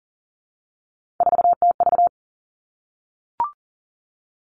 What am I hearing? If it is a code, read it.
Morse code, heard as a keyed tone, about 38 wpm, 700 Hz. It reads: 4TV
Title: Three Challenging Letter-Number Combinations 40wpm